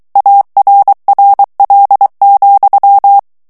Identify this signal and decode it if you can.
Morse code, heard as a keyed tone, about 23 wpm, 800 Hz. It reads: ARRL,
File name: ANGLAIS texte clair 3 vitesse 22 mots minute